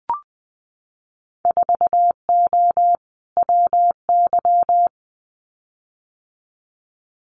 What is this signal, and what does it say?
Morse code, heard as a keyed tone, about 20 wpm, 700 Hz. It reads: 4O WY
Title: Winter Field Day Exchanges Encore 20wpm